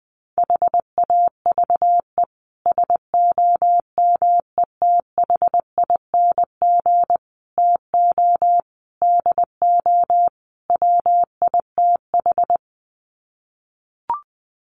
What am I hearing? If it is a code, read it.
Morse code, heard as a keyed tone, about 20 wpm, 700 Hz. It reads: HAVE SOMETHING TO DO WITH
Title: Sets of 5 Words Encore 20wpm